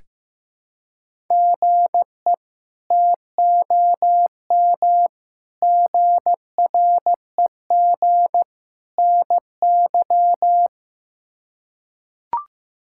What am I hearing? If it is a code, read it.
Morse code, heard as a keyed tone, about 15 wpm, 700 Hz. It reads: GE TOM GREG NY
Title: K1USN SST QSO Practice - 15wpm